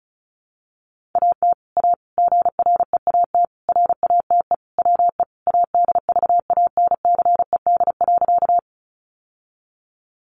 Morse code, heard as a keyed tone, about 34 wpm, 700 Hz. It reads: AT A GREAT RATE WE ADVANCED.
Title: Sentences from Top 1200 Words 35wpm